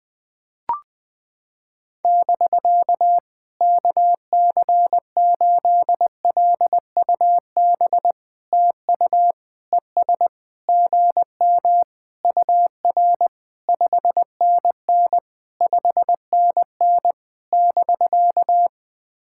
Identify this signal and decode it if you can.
Morse code, heard as a keyed tone, about 18 wpm, 700 Hz. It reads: <BK> KC8LUB TU ES GM UR 5NN 5NN <BK>
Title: POTA QSO Practice - 20wpm